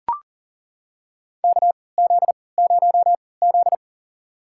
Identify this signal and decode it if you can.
Morse code, heard as a keyed tone, about 40 wpm, spaced as for 23 wpm, 700 Hz. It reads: KZ0Z
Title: USA Call Signs - 4 Characters - 3x Character Spacing - 40wpm